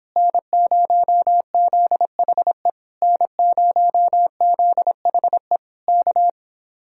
Morse code, heard as a keyed tone, about 26 wpm, 700 Hz. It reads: N0ZHE N0ZHE K